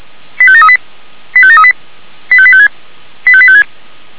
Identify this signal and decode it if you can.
Five-tone selective call (ZVEI), heard as quick runs of five tones, tones of about 70 ms. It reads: 85738 85738 85856 85856